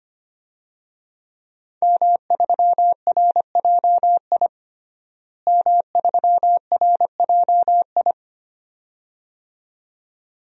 Morse code, heard as a keyed tone, about 25 wpm, 700 Hz. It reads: M3RJI M3RJI
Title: DXCC-100 Call Signs - 5 Characters - Repeated 2x - 25wpm